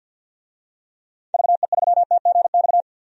Morse code, heard as a keyed tone, about 52 wpm, 700 Hz. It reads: VE3TGX
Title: DXCC-100 Call Signs - 6 Characters - 50wpm